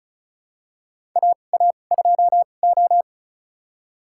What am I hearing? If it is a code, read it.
Morse code, heard as a keyed tone, about 36 wpm, spaced as for 25 wpm, 700 Hz. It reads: AA2O
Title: USA Call Signs - 4 Characters - 2x Character Spacing - 35wpm